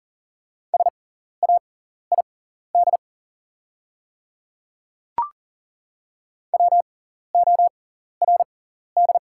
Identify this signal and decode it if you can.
Morse code, heard as a keyed tone, about 40 wpm, spaced as for 14 wpm, 690 Hz. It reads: SAID WORD
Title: 6x Character Spacing - Top 100 Words - 40wpm